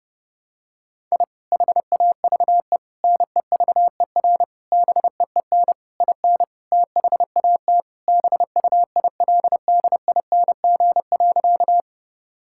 Morse code, heard as a keyed tone, about 32 wpm, 700 Hz. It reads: I HAVE NEVER BEEN IN THAT BUILDING.
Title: Sentences from Top 1000 Words - Review Part 5 - 30wpm